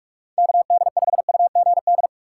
Morse code, heard as a keyed tone, about 47 wpm, 700 Hz. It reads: KD5UGD
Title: USA Call Signs - 6 Characters - 2x Character Spacing - 45wpm